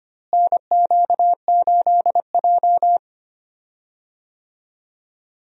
Morse code, heard as a keyed tone, about 25 wpm, 700 Hz. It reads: NQ8J
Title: USA Call Signs - 4 Characters - Repeated 2x - 25wpm